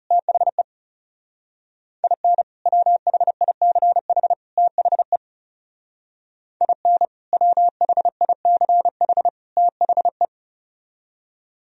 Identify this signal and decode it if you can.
Morse code, heard as a keyed tone, about 38 wpm, 700 Hz. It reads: THE IN WHICH THE IN WHICH THE
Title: Speed-Racing - Sets of 3 Words 30wpm